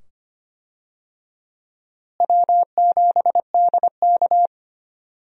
Morse code, heard as a keyed tone, about 25 wpm, 700 Hz. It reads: W7DK
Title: USA Call Signs - 4 Characters - Repeated 2x - 25wpm